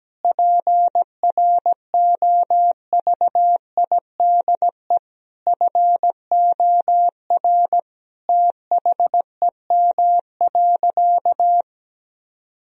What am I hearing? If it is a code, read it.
Morse code, heard as a keyed tone, about 17 wpm, 700 Hz. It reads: PROVIDE FOR THEM.
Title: Sentences from Top 1000 Words - Review Part 3 - 17wpm